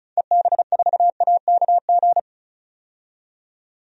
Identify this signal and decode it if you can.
Morse code, heard as a keyed tone, about 34 wpm, 700 Hz. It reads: EB4AKG
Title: DXCC-100 Call Signs - 6 Characters - 35wpm